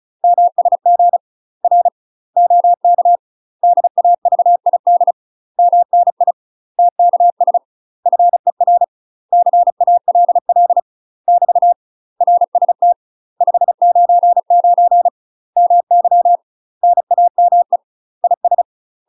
Morse code, heard as a keyed tone, about 34 wpm, 700 Hz. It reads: MSG R OK DAVID MNI TKS FER CALL <BT> RST 599 MY NAME IS
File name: UK MASTER QSOs TRAINING / 35WPM / UK MASTER QSOs 0003